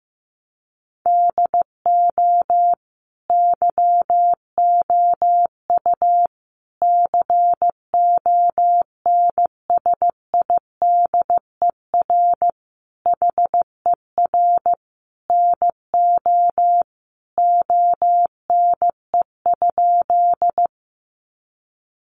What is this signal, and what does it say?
Morse code, heard as a keyed tone, about 15 wpm, 700 Hz. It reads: DO YOU CONSIDER HER NO ONE?
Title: Sentences from Top 600 Words 15wpm